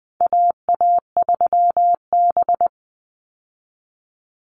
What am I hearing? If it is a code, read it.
Morse code, heard as a keyed tone, about 20 wpm, 700 Hz. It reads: AA3B
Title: Call Signs Repeated 3x - 20wpm